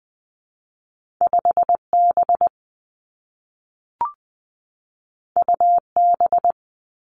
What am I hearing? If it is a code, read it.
Morse code, heard as a keyed tone, about 20 wpm, 700 Hz. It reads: 5 B U B
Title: Two Challenging Letter-Number Combinations 20wpm